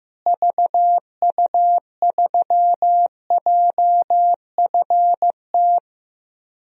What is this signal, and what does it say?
Morse code, heard as a keyed tone, about 15 wpm, 700 Hz. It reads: VU3JFT
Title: DXCC-100 Call Signs - 6 Characters - Repeated 3x - 15wpm